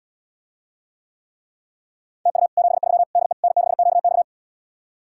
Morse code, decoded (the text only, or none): TU 73 DE N6QX